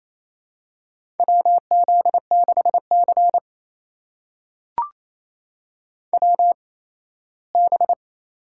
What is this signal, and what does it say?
Morse code, heard as a keyed tone, about 28 wpm, spaced as for 17 wpm, 700 Hz. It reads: WZ6C WB
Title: USA Call Signs - 4 Characters - 8x Character Spacing - 28wpm